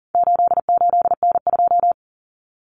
Morse code, heard as a keyed tone, about 40 wpm, 700 Hz. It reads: 88N2
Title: Common Cut Numbers - 4-Digit 40wpm